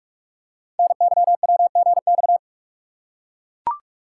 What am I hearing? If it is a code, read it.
Morse code, heard as a keyed tone, about 44 wpm, 700 Hz. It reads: NYWGX
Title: Code Groups - 5 Letter:Numbers - 45wpm